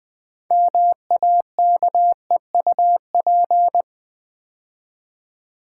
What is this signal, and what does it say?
Morse code, heard as a keyed tone, about 21 wpm, 700 Hz. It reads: MAKEUP
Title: 6 Letter Words - First 3 Letters Given - 20wpm